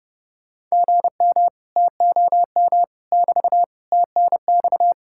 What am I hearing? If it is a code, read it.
Morse code, heard as a keyed tone, about 39 wpm, 700 Hz. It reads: GM TOM <BT> TNX